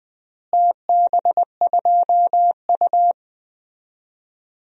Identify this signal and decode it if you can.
Morse code, heard as a keyed tone, about 21 wpm, 700 Hz. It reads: TB2U
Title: DXCC-100 Call Signs - 4 Characters - Repeated 2x - 20wpm